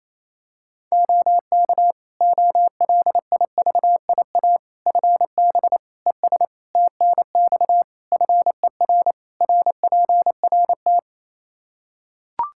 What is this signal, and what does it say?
Morse code, heard as a keyed tone, about 28 wpm, 700 Hz. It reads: OK OLIVIA FB ES TNX FER RPRT
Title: Practice QSO - Exchange 2 - Info - 28wpm